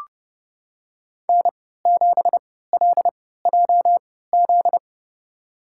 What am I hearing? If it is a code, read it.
Morse code, heard as a keyed tone, about 29 wpm, spaced as for 17 wpm, 700 Hz. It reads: N7LJZ
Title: USA Call Signs - 5 Characters - 3x Character Spacing - 30wpm